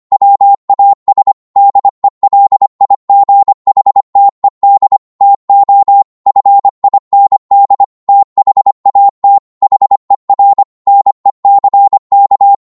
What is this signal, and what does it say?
Morse code, heard as a keyed tone, about 25 wpm, 800 Hz. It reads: WAS DELIGHTED TO FIND THAT HER NECK
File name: Morse / psg-25wpm